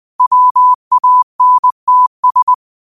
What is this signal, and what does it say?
Morse code, heard as a keyed tone, about 20 wpm, 1000 Hz. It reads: WANTS
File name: 20wpm